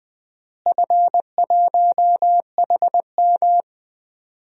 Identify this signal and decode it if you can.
Morse code, heard as a keyed tone, about 20 wpm, 700 Hz. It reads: F1HM
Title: DXCC-100 Call Signs - 4 Characters - Repeated 2x - 20wpm